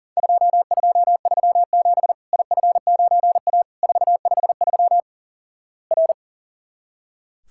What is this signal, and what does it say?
Morse code, heard as a keyed tone, about 37 wpm, 700 Hz. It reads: 2237 IF9A 453 R
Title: Worked All Europe DX Contest - QTC Exchange - 40wpm